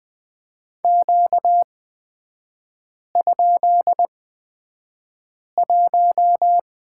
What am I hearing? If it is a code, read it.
Morse code, heard as a keyed tone, about 20 wpm, spaced as for 5 wpm, 690 Hz. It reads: Q?1
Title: 236 Lesson 38e - Letter q - Three Characters 20wpm